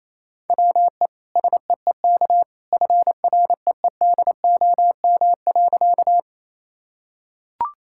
Morse code, heard as a keyed tone, about 28 wpm, 700 Hz. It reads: WE SEEK FREEDOM.
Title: Sentences from Top 1600 Words 28wpm